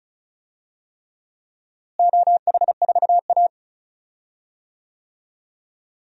Morse code, heard as a keyed tone, about 36 wpm, 700 Hz. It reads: OH4A
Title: DXCC-100 Call Signs - 4 Characters - Repeated 3x - 35wpm